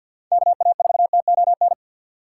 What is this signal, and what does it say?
Morse code, heard as a keyed tone, about 49 wpm, 700 Hz. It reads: KA4TON
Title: USA Call Signs - 6 Characters - 4x Character Spacing - 50wpm